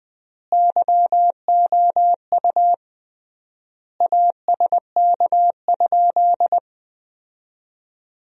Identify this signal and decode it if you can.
Morse code, heard as a keyed tone, about 20 wpm, 700 Hz. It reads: YOU ASK?
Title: 3x Word-Spacing - Sentences from Top 200 Words - 20wpm